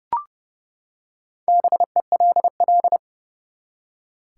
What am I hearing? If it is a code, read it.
Morse code, heard as a keyed tone, about 30 wpm, 700 Hz. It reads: BELL
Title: Four Letter Words - 30wpm